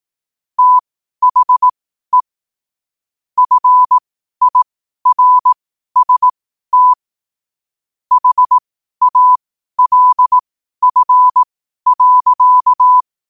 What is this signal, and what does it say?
Morse code, heard as a keyed tone, about 18 wpm, spaced as for 12 wpm, 1000 Hz. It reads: THE FIRST HALF.
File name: qotdfiles / 12wpm